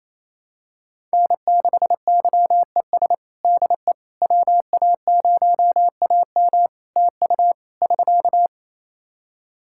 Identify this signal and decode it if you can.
Morse code, heard as a keyed tone, about 27 wpm, 700 Hz. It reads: N6YES DE WA0AM TU <SK>
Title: Practice QSO - Exchange 4 - Ending - 28wpm